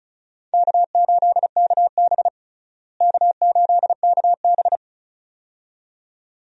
Morse code, heard as a keyed tone, about 34 wpm, 700 Hz. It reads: K8KB K8KB
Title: USA Call Signs - 4 Characters - Repeated 2x - 35wpm